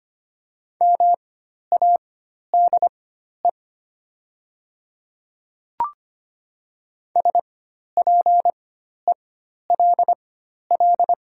Morse code, heard as a keyed tone, about 25 wpm, spaced as for 12 wpm, 700 Hz. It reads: MADE SPELL
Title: 4x Character Spacing - Top 200 Words - 25wpm